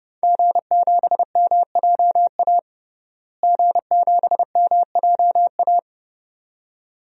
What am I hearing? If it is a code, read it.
Morse code, heard as a keyed tone, about 30 wpm, 700 Hz. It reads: G7MJA G7MJA
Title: DXCC-100 Call Signs - 5 Characters - Repeated 2x - 30wpm